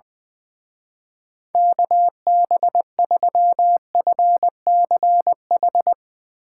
Morse code, heard as a keyed tone, about 20 wpm, 700 Hz. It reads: KB3FCH
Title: Speed Racing - USA 2x3 Call Signs 20wpm